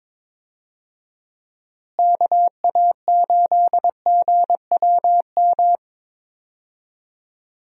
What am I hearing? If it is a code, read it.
Morse code, heard as a keyed tone, about 22 wpm, 700 Hz. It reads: KA8GWM
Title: USA Call Signs - 6 Characters - Repeated 2x - 22wpm